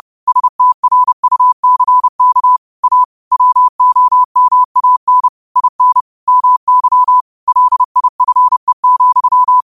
Morse code, heard as a keyed tone, about 30 wpm, 1000 Hz. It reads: STRUCK A WOMAN IN MY LIFE,
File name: qotdfiles / 30wpm